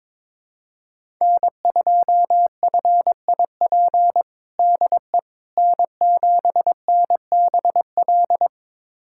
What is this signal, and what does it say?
Morse code, heard as a keyed tone, about 22 wpm, 700 Hz. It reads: N2FIP DE N7NBL
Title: USA Callsign DE Callsign - 5 Characters - 22wpm